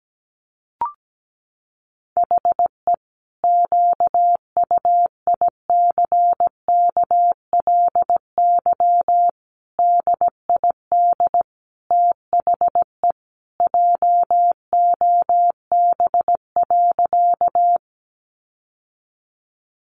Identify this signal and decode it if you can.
Morse code, heard as a keyed tone, about 17 wpm, 700 Hz. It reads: HE QUICKLY DID THE JOB.
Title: Sentences from Top 700 Words 17wpm